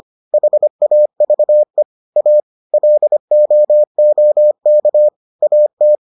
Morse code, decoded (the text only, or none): HAVE A LOOK AT